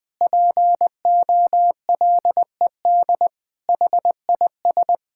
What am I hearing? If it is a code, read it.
Morse code, heard as a keyed tone, about 20 wpm, 700 Hz. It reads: POLED HIS
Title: Pangram Challenge v2 - 20wpm